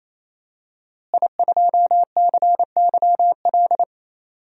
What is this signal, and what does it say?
Morse code, heard as a keyed tone, about 28 wpm, 700 Hz. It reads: I2CYL
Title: DXCC-100 Call Signs - 5 Characters - Repeated 3x - 28wpm